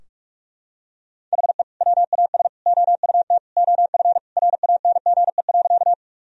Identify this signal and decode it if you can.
Morse code, heard as a keyed tone, about 46 wpm, 700 Hz. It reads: HE WAS OUT OF RANGE.
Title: Sentences from Top 1000 Words - Review Part 4 - 45wpm